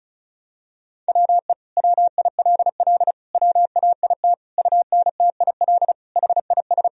Morse code, heard as a keyed tone, about 34 wpm, 700 Hz. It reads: WE WILL WAIT UNTIL HIS
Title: Sentences from Top 1000 Words - Review Part 1 - 35wpm